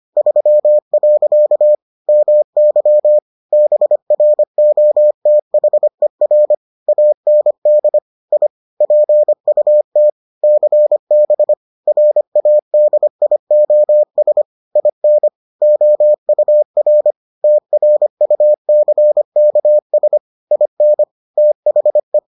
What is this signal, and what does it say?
Morse code, heard as a keyed tone, about 25 wpm, 600 Hz. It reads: 3. MY BROTHER AND I PUT CB RADIOS IN OUR TRUCKS IN THE